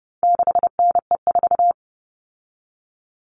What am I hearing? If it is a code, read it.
Morse code, heard as a keyed tone, about 30 wpm, 700 Hz. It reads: 6NE4
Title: All Cut Numbers - 4-Digit 30wpm